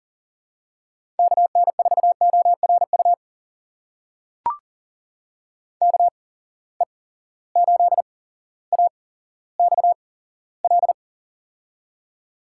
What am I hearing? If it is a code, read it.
Morse code, heard as a keyed tone, about 39 wpm, spaced as for 17 wpm, 700 Hz. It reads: KN4ORU KE8AXL
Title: USA Call Signs - 6 Characters - 8x Character Spacing - 40wpm